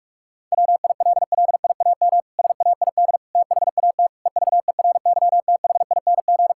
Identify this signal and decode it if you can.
Morse code, heard as a keyed tone, about 46 wpm, 700 Hz. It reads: WILLIAM SAID THAT EVERYTHING
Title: Pangram Challenge v2 - 45wpm